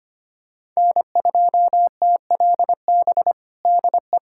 Morse code, heard as a keyed tone, about 25 wpm, 700 Hz. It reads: N2TLB DE
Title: USA Callsign DE Callsign - 5 Characters - 25wpm